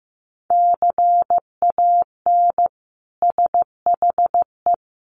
Morse code, heard as a keyed tone, about 15 wpm, 700 Hz. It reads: CAN SHE
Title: Sentences from Top 1100 Words 15wpm